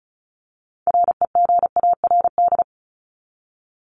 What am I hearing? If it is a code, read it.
Morse code, heard as a keyed tone, about 34 wpm, 700 Hz. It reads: REGARD
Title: ING Suffix - Top 100 Words with Base Words 35wpm